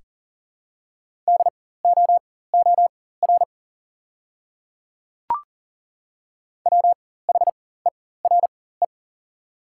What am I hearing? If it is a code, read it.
Morse code, heard as a keyed tone, about 37 wpm, spaced as for 18 wpm, 700 Hz. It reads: DOOR WHERE
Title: 4x Character Spacing - Top 300 Words - 40wpm